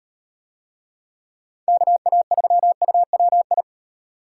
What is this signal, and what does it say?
Morse code, heard as a keyed tone, about 35 wpm, 700 Hz. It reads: KA3UWI